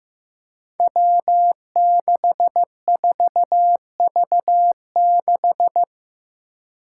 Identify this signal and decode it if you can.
Morse code, heard as a keyed tone, about 15 wpm, 700 Hz. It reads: W64V6
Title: Five Challenging Letter-Number Combinations 15wpm